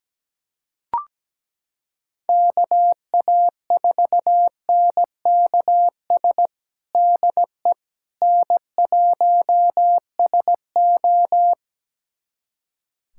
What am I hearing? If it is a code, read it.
Morse code, heard as a keyed tone, about 17 wpm, 700 Hz. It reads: KA4NKS DE N1SO
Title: USA Callsign DE Callsign 17wpm